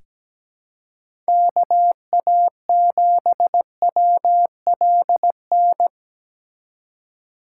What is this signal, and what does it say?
Morse code, heard as a keyed tone, about 17 wpm, 700 Hz. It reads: KA7WLN
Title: USA Call Signs - 6 Characters - 3x Character Spacing - 17wpm